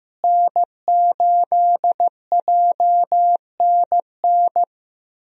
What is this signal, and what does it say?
Morse code, heard as a keyed tone, about 15 wpm, 700 Hz. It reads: N8JNN
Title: USA Call Signs - 5 Characters - Repeated 3x - 15wpm